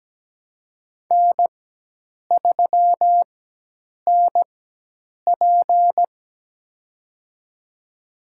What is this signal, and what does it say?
Morse code, heard as a keyed tone, about 17 wpm, spaced as for 8 wpm, 700 Hz. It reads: N3NP
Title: USA Call Signs - 4 Characters - 4x Character Spacing - 17wpm